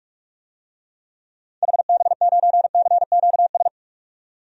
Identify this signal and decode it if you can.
Morse code, heard as a keyed tone, about 46 wpm, 700 Hz. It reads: HB9CQS
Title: DXCC-100 Call Signs - 6 Characters - 45wpm